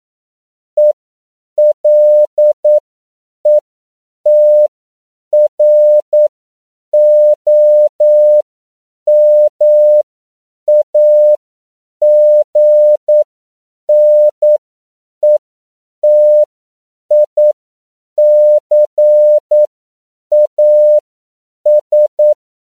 Morse code, heard as a keyed tone, about 9 wpm, spaced as for 7 wpm, 600 Hz. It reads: ELETROMAGNETICAS
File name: CW / 10PPM